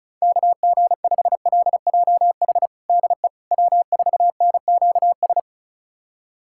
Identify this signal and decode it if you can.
Morse code, heard as a keyed tone, about 36 wpm, 690 Hz. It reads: KG5LJH DE W4NQS
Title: USA Callsign DE Callsign 35wpm